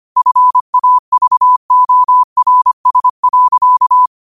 Morse code, heard as a keyed tone, about 25 wpm, 1000 Hz. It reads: FAVORS.
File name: qotdfiles / 25wpm